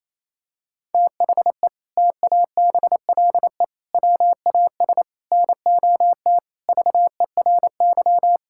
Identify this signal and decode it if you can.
Morse code, heard as a keyed tone, about 28 wpm, 700 Hz. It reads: THE TABLE WAS NOT VERY
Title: Sentences from Top 1000 Words - Review Part 6 - 28wpm